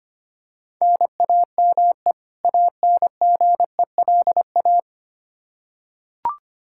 Morse code, heard as a keyed tone, about 25 wpm, 700 Hz. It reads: NAME ANGELA
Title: K1USN SST QSO Practice 25wpm